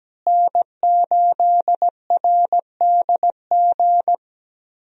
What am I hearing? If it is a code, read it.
Morse code, heard as a keyed tone, about 17 wpm, 700 Hz. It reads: N8RDG
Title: USA Call Signs - 5 Characters - Repeated 2x - 17wpm